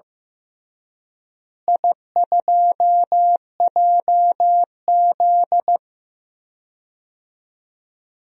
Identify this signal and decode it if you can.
Morse code, heard as a keyed tone, about 16 wpm, 700 Hz. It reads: I2JZ
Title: DXCC-100 Call Signs - 4 Characters - Repeated 3x - 15wpm